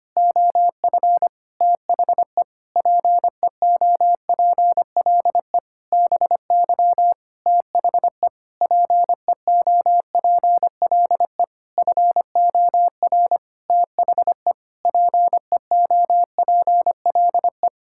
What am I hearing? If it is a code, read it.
Morse code, heard as a keyed tone, about 25 wpm, 700 Hz. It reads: OF THE PEOPLE BY THE PEOPLE FOR THE PEOPLE
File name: practice / 25wpm / Sentences with 9-Word Phrases - 25wpm